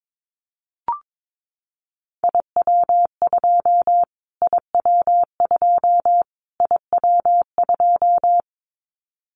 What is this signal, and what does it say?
Morse code, heard as a keyed tone, about 22 wpm, 700 Hz. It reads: IW2 IW2 IW2
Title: Callsign Prefixes for Top 25 DXCC - Repeated 3x - 22wpm